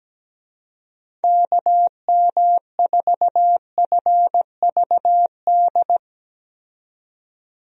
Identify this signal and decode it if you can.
Morse code, heard as a keyed tone, about 17 wpm, 700 Hz. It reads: KM4FVD
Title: USA Call Signs - 6 Characters - 17wpm